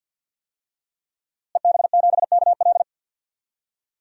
Morse code, heard as a keyed tone, about 53 wpm, 700 Hz. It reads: EB7KL
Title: DXCC-100 Call Signs - 5 Characters - Repeated 2x - 50wpm